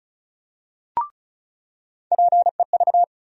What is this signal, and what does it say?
Morse code, heard as a keyed tone, about 36 wpm, 700 Hz. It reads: PEV
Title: USA Call Sign Postfixes - 3 Letters 35wpm